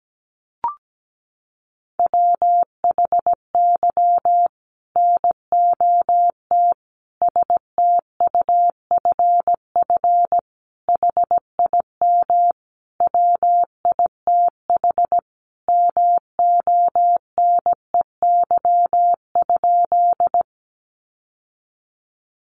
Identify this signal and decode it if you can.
Morse code, heard as a keyed tone, about 17 wpm, 700 Hz. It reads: WHY NOT STUFF HIM WITH MONEY?
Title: Sentences from Top 1400 Words 17wpm